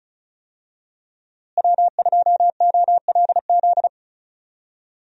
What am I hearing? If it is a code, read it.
Morse code, heard as a keyed tone, about 34 wpm, 700 Hz. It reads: W2OLZ
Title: USA Call Signs - 5 Characters - Repeated 3x - 35wpm